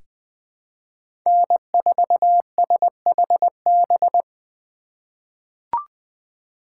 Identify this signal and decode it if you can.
Morse code, heard as a keyed tone, about 20 wpm, 700 Hz. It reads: N4SHB
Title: USA Call Signs - 2x Character Spacing - 20wpm